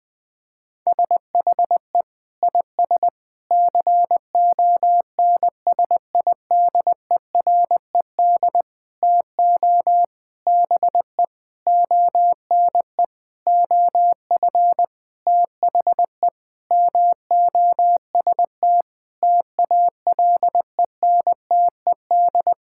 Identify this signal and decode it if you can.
Morse code, heard as a keyed tone, about 20 wpm, 700 Hz. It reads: SHE IS CONSIDERED TO BE ONE OF THE MOST TALENTED
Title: Sentences with 8-Word Phrases - Spoken Beginning - 20wpm